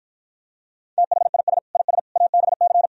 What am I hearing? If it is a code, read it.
Morse code, heard as a keyed tone, about 52 wpm, 700 Hz. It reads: THIS IS ABC